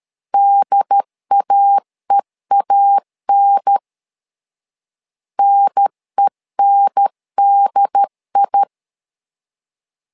Morse code, heard as a keyed tone, about 13 wpm, 800 Hz. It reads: DAEAN NENDI
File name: LEZIONI / 2 UHDN / 2 UHDN 12wpm 1456